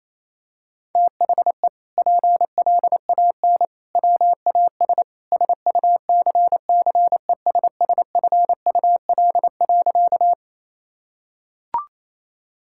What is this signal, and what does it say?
Morse code, heard as a keyed tone, about 28 wpm, 700 Hz. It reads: THE PLAN WAS SUCCESSFUL.
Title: Sentences from Top 1000 Words - Review Part 6 - 28wpm